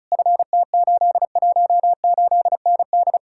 Morse code, heard as a keyed tone, about 48 wpm, spaced as for 42 wpm, 700 Hz. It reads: FT818ND
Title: Speed-Racing Common QSO Phrases - 35wpm